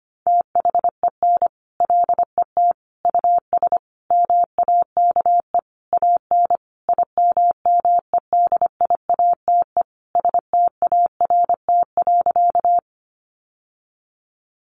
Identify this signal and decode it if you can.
Morse code, heard as a keyed tone, about 25 wpm, 700 Hz. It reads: THEN LET US MAKE AN IMMEDIATE START.
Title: Sentences from Top 1200 Words 25wpm